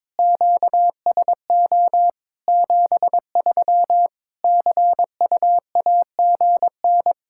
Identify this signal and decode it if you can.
Morse code, heard as a keyed tone, about 22 wpm, 700 Hz. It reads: QSO 73 CUAGN
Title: Practice QSO - Exchange 4 - Ending - 22wpm